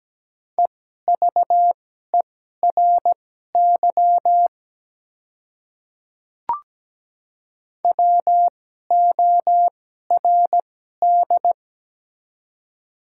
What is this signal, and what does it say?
Morse code, heard as a keyed tone, about 17 wpm, spaced as for 12 wpm, 690 Hz. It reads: EVERY WORD
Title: 2x Character Spacing - Top 300 Words - 17wpm